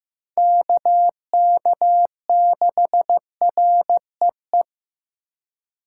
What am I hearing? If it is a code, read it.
Morse code, heard as a keyed tone, about 18 wpm, 700 Hz. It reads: KK6REE
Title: Speed Racing - USA 2x3 Call Signs 15wpm